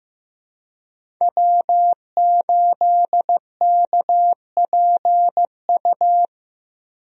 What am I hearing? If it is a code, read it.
Morse code, heard as a keyed tone, about 15 wpm, 700 Hz. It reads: W8KPU
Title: USA Call Signs - 5 Characters - Repeated 2x - 15wpm